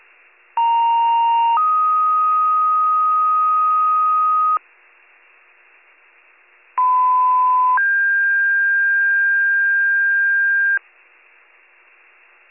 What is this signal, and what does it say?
Two-tone page, heard as a one-second tone, then a three-second tone, about 930 then 1250 Hz, about 1010 then 1600 Hz. It is heard from 0.6 to 10.8 s.